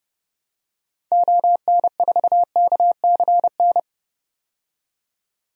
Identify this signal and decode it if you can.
Morse code, heard as a keyed tone, about 30 wpm, 700 Hz. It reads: ON4KCN
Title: DXCC-100 Call Signs - 6 Characters - 30wpm